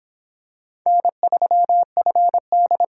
Morse code, heard as a keyed tone, about 25 wpm, 700 Hz. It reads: N3FD